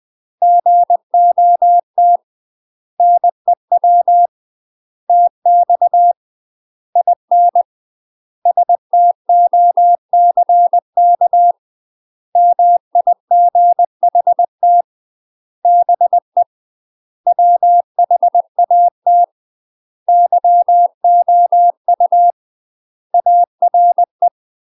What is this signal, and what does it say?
Morse code, heard as a keyed tone, about 20 wpm, 700 Hz. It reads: GOT NEW TX IN STOCK MIGHT BE WHAT YOU ARE
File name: UK MASTER QSOs TRAINING / 20WPM / UK MASTER QSOs 0031